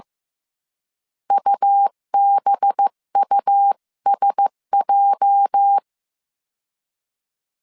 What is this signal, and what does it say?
Morse code, heard as a keyed tone, about 15 wpm, 800 Hz. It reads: UBUSJ
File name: CW / DET / 1 9 1500C S B 14wpm 3302